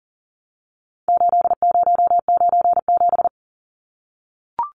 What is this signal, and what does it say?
Morse code, heard as a keyed tone, about 40 wpm, 700 Hz. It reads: 8097
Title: Numbers - 4-Digit 40wpm